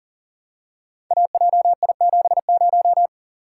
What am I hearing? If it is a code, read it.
Morse code, heard as a keyed tone, about 41 wpm, 700 Hz. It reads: AJI70
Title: Code Groups - 5 Letter:Numbers - 40wpm